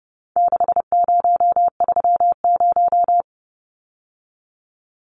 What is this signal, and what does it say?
Morse code, heard as a keyed tone, about 30 wpm, 700 Hz. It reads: 6030
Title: Numbers - 4-Digit 30wpm